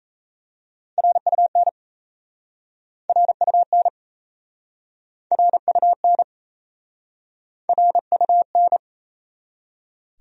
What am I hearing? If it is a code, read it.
Morse code, heard as a keyed tone, about 37 wpm, 690 Hz. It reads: RUN RUN RUN RUN